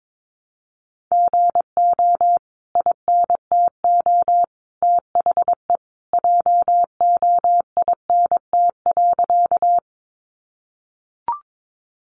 Morse code, heard as a keyed tone, about 22 wpm, 700 Hz. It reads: GO INTO THE JOINT.
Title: Sentences from Top 1300 Words 22wpm